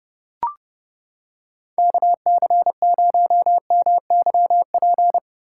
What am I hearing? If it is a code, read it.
Morse code, heard as a keyed tone, about 29 wpm, 700 Hz. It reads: KC0MYP